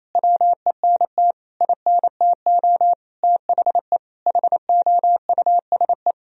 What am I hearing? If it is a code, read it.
Morse code, heard as a keyed tone, about 28 wpm, 700 Hz. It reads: WENT INTO THE HOUSE
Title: Sentences from Top 1000 Words - Review Part 1 - 28wpm